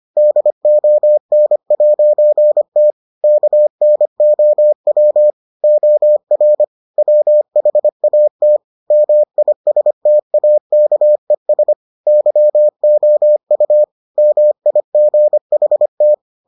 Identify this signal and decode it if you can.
Morse code, heard as a keyed tone, about 25 wpm, 600 Hz. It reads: DON'T KNOW OR WHAT MISTAKES YOU MIGHT